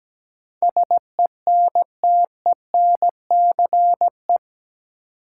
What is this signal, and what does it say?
Morse code, heard as a keyed tone, about 17 wpm, 700 Hz. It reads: SENTENCE
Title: Sentences from Top 1000 Words - Review Part 1 - New Word Drill - 17wpm